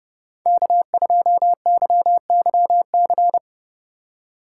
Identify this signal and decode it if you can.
Morse code, heard as a keyed tone, about 30 wpm, 700 Hz. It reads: K2YYC